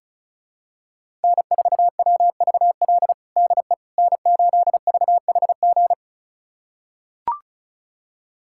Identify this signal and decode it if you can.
Morse code, heard as a keyed tone, about 36 wpm, 700 Hz. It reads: N4WVL DE N8VHG
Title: USA Callsign DE Callsign - 5 Characters - 35wpm